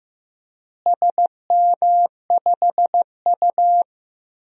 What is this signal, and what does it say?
Morse code, heard as a keyed tone, about 16 wpm, 700 Hz. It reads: SM5U
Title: DXCC-100 Call Signs - 4 Characters - Repeated 3x - 15wpm